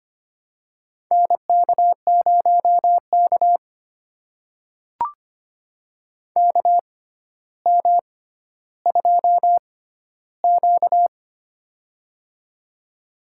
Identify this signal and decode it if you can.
Morse code, heard as a keyed tone, about 25 wpm, spaced as for 13 wpm, 700 Hz. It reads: NK0K KM2Q
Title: USA Call Signs - 4 Characters - 6x Character Spacing - 25wpm